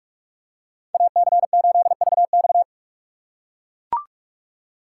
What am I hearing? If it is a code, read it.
Morse code, heard as a keyed tone, about 44 wpm, 690 Hz. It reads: AC8VX
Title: USA Call Signs - 5 Characters - 6x Character Spacing - 45wpm